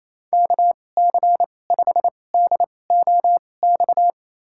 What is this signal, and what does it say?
Morse code, heard as a keyed tone, about 29 wpm, spaced as for 20 wpm, 690 Hz. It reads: KC5DOX
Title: USA Call Signs - 2x Character Spacing - 28wpm